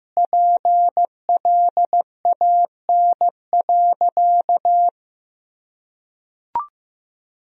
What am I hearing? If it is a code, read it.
Morse code, heard as a keyed tone, about 15 wpm, 700 Hz. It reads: PLAN.
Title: Sentences from Top 500 Words and Abbreviations 15wpm